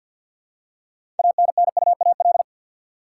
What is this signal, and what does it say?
Morse code, heard as a keyed tone, about 52 wpm, 700 Hz. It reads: ANNUAL
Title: Six Letter Words - 50wpm